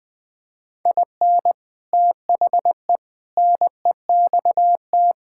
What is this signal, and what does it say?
Morse code, heard as a keyed tone, about 20 wpm, 700 Hz. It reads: IN THE NEXT
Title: Sentences with 3-Word Phrases - 20wpm